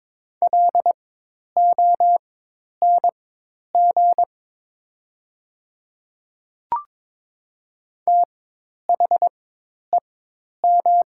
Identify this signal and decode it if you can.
Morse code, heard as a keyed tone, about 22 wpm, spaced as for 10 wpm, 700 Hz. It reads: LONG THEM
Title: 4x Character Spacing - Top 100 Words - 22wpm